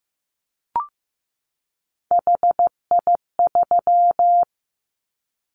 Morse code, heard as a keyed tone, about 15 wpm, 700 Hz. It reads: HI3
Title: Callsign Prefixes for Top 100 DXCC - 15wpm